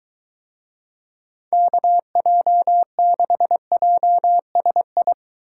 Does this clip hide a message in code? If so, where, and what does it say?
Yes: Morse code from 1.5 to 5.1 s, heard as a keyed tone, about 23 wpm, 700 Hz. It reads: KJ6JSI